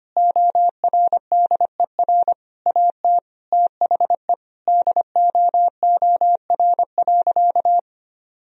Morse code, heard as a keyed tone, about 25 wpm, 700 Hz. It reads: ORDER AT THE DOOR.
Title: Sentences from Top 1000 Words - Review Part 2 - 25wpm